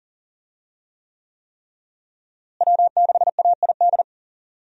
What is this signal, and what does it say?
Morse code, heard as a keyed tone, about 40 wpm, 700 Hz. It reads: W6AID
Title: USA Call Signs - 5 Characters - Repeated 3x - 40wpm